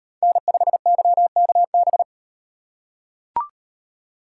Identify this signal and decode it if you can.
Morse code, heard as a keyed tone, about 34 wpm, 700 Hz. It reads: N5YKB